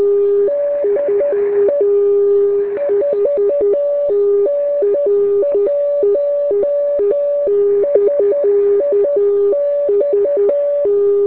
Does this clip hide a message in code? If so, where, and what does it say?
Yes: Morse code from 0.5 to 10.9 s, heard as a keyed tone, about 10 wpm, 570 Hz, with noise. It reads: DE 4N1SIX